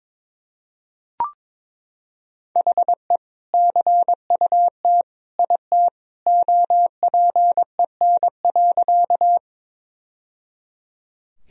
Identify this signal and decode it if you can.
Morse code, heard as a keyed tone, about 22 wpm, 700 Hz. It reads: HE CUT IT OPEN.
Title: Sentences from Top 1000 Words - Review Part 1 - 22wpm